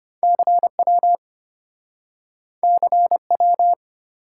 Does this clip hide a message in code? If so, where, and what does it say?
Yes: Morse code from 0.2 to 3.7 s, heard as a keyed tone, about 34 wpm, 700 Hz. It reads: CW CW